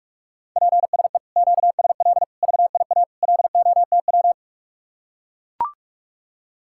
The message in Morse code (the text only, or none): PSE QSL VIA LOTW